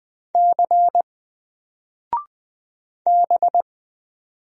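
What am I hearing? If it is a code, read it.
Morse code, heard as a keyed tone, about 20 wpm, 700 Hz. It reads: C B
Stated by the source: Single letter and number Morse drill at 20 wpm, from N4LCD.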